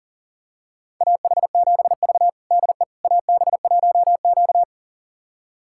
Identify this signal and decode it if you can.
Morse code, heard as a keyed tone, about 40 wpm, 700 Hz. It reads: AH7V DE AB1Q
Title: USA Callsign DE Callsign - 4 Characters - 40wpm